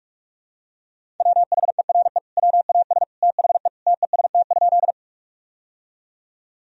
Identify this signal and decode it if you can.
Morse code, heard as a keyed tone, about 42 wpm, 700 Hz. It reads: WHERE WAS THE TEST?
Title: Sentences from Top 1000 Words - Review Part 3 - 45wpm